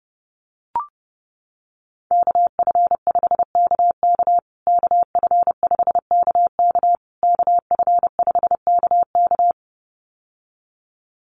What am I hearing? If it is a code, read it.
Morse code, heard as a keyed tone, about 30 wpm, 700 Hz. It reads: KF5KK KF5KK KF5KK
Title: USA Call Signs Repeated 3x 30wpm